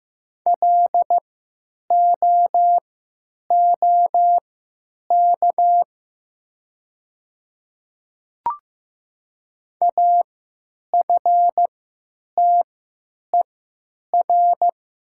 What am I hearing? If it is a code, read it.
Morse code, heard as a keyed tone, about 15 wpm, spaced as for 9 wpm, 700 Hz. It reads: LOOK AFTER
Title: 3x Character Spacing - Top 500 Words - 15wpm